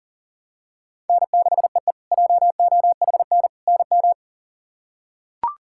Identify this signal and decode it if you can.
Morse code, heard as a keyed tone, about 40 wpm, 700 Hz. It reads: N6EE JOHN NM
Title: K1USN SST QSO Practice - 40wpm